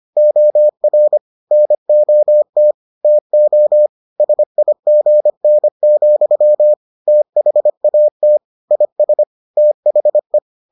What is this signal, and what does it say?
Morse code, heard as a keyed tone, about 25 wpm, 600 Hz. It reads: OR NOT TO SIGN, THAT IS THE